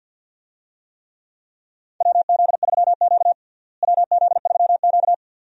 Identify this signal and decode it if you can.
Morse code, heard as a keyed tone, about 51 wpm, 700 Hz. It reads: WZ3Q WZ3Q
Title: USA Call Signs - 4 Characters - Repeated 2x - 50wpm